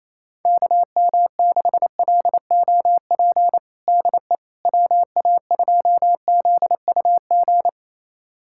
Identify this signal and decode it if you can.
Morse code, heard as a keyed tone, about 28 wpm, 700 Hz. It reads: KM6LOP DE WA2ZUG
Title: USA Callsign DE Callsign - 6 Characters - 28wpm